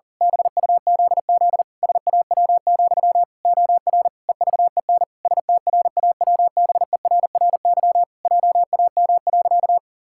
Morse code, heard as a keyed tone, about 40 wpm, 700 Hz. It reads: BUZZ SAW, OR EVEN STRAWBERRY JAM.
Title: Pangram Challenge v2 - 40wpm